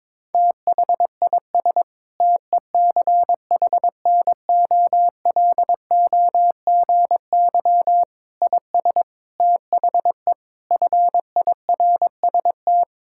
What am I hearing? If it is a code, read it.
Morse code, heard as a keyed tone, about 22 wpm, 700 Hz. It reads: THIS TECHNOLOGY IS THE FIRST
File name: practice / 22wpm / Sentences with 8-Word Phrases - 22wpm